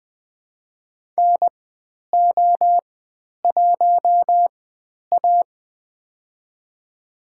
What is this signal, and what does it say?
Morse code, heard as a keyed tone, about 20 wpm, spaced as for 10 wpm, 700 Hz. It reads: NO1A
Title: 095 Lesson 16g - Letter c - Call Signs 20wpm